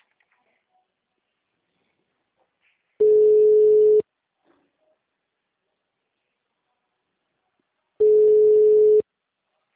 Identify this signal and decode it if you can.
Ringback tone (Europe, 425 Hz).
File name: out- -2002- - - .2